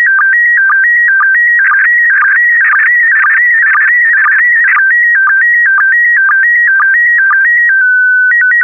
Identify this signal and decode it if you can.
SSTV (slow-scan television), PD 120.
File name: ISS-SSTV / SSTV-Audio-ISS-Bild-07